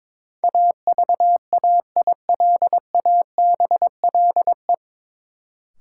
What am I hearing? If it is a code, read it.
Morse code, heard as a keyed tone, about 21 wpm, 700 Hz. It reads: AVAILABLE
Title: Sentences from Top 600 Words - New Word Drill - 22wpm